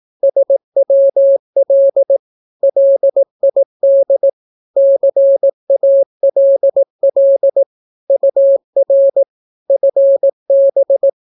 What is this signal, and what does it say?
Morse code, heard as a keyed tone, about 18 wpm, 550 Hz. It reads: SWL LID CALL UR FB
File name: naukapodstaw / skroty / skroty